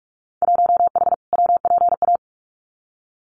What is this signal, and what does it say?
Morse code, heard as a keyed tone, about 46 wpm, 700 Hz. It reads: 1H WPA
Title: Winter Field Day Exchanges 45wpm